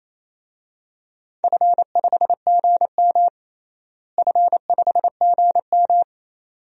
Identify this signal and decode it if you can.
Morse code, heard as a keyed tone, about 28 wpm, 700 Hz. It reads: F5GM F5GM
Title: DXCC-100 Call Signs - 4 Characters - Repeated 2x - 28wpm